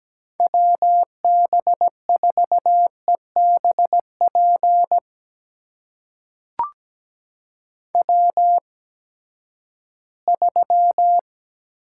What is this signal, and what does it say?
Morse code, heard as a keyed tone, about 17 wpm, spaced as for 12 wpm, 700 Hz. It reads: WB4EBP W3
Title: USA Call Signs - 8x Character Spacing - 17wpm